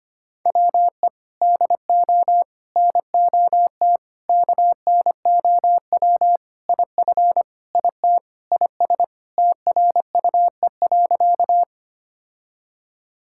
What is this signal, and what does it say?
Morse code, heard as a keyed tone, about 25 wpm, 700 Hz. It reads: WE DO NOT KNOW IF IT IS TRUE.
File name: practice / 25wpm / Sentences with 4-Word Phrases - 25wpm